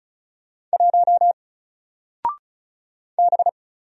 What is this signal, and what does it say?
Morse code, heard as a keyed tone, about 36 wpm, 700 Hz. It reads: 1 B
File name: Single-Letter-Number-35wpm